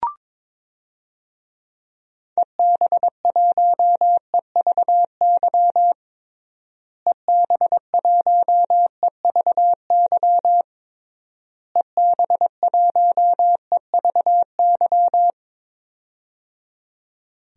Morse code, heard as a keyed tone, about 22 wpm, 700 Hz. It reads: EB1EVY EB1EVY EB1EVY
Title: DXCC-100 Call Signs - 6 Characters - Repeated 3x - 22wpm